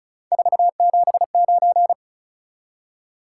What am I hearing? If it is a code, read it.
Morse code, heard as a keyed tone, about 36 wpm, 700 Hz. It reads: 479
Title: Top 180 QSO Elements 35wpm